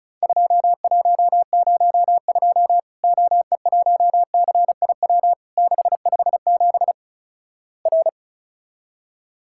Morse code, heard as a keyed tone, about 36 wpm, 700 Hz. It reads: 2102 OE1CIW 657 R
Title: Worked All Europe DX Contest - QTC Exchange - 35wpm